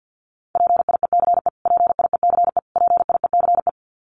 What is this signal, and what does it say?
Morse code, heard as a keyed tone, about 48 wpm, 700 Hz. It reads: PIECE PIECE PIECE
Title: Top 500 Words Repeated 50wpm